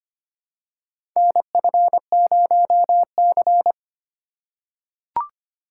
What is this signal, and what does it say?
Morse code, heard as a keyed tone, about 25 wpm, 700 Hz. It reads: NF0C
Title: USA Call Signs - 4 Characters - 2x Character Spacing - 25wpm